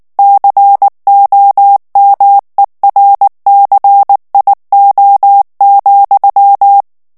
Morse code, heard as a keyed tone, about 19 wpm, 800 Hz. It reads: COMERCIO,
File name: morse mp3 / ESPAGNOL TEXTE EN CLAIR 2 VITESSE 18 MOTS MINUTES